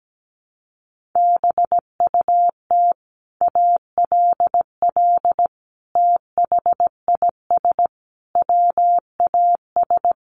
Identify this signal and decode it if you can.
Morse code, heard as a keyed tone, about 17 wpm, 700 Hz. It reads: BUT ALL THIS WAS
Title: Sentences from Top 1400 Words 17wpm